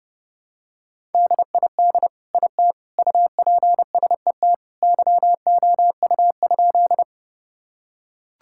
Morse code, heard as a keyed tone, about 30 wpm, 700 Hz. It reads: DID IT UPSET YOU?
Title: Sentences from Top 1000 Words - Review Part 5 - 30wpm